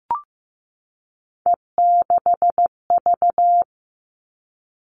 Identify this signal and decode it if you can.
Morse code, heard as a keyed tone, about 15 wpm, 700 Hz. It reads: E6V
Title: All Cut Numbers - 3-Digit 15wpm